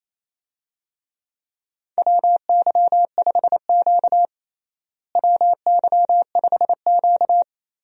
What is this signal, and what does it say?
Morse code, heard as a keyed tone, about 28 wpm, 700 Hz. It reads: WY5Q WY5Q
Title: USA Call Signs - 4 Characters - Repeated 2x - 28wpm